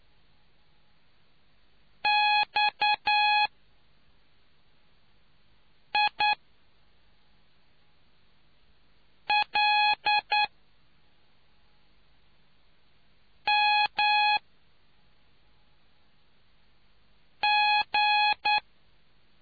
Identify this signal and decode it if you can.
Morse code, heard as a keyed tone, about 9 wpm, 840 Hz. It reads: X I L M G